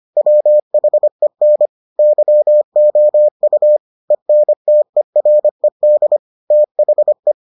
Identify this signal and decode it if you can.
Morse code, heard as a keyed tone, about 25 wpm, 600 Hz. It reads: WHEN YOU ENTERED THE